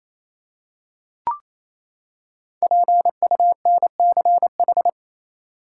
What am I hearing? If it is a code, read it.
Morse code, heard as a keyed tone, about 28 wpm, 700 Hz. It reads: PUNCH
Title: Five Letter Words - 28wpm